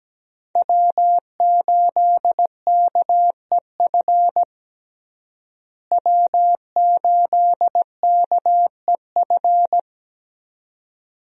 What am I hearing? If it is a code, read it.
Morse code, heard as a keyed tone, about 17 wpm, 700 Hz. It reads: W8KEF W8KEF
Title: USA Call Signs - 5 Characters - Repeated 2x - 17wpm